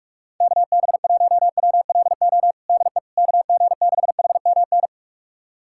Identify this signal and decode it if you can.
Morse code, heard as a keyed tone, about 46 wpm, 700 Hz. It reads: KB1WLO DE KG6HMN
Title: USA Callsign DE Callsign 45wpm